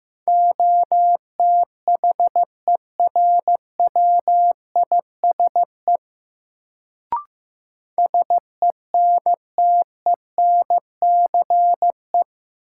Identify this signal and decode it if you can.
Morse code, heard as a keyed tone, about 16 wpm, 700 Hz. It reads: OTHERWISE SENTENCE
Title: Sentences from Top 1200 Words - New Word Drill Alt - 15wpm